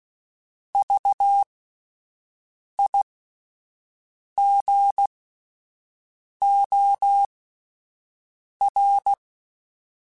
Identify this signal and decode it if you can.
Morse code, heard as a keyed tone, about 16 wpm, spaced as for 5 wpm, 800 Hz. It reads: VIGOR
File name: Alphabetical / Alphabetical mp3s / Vs @5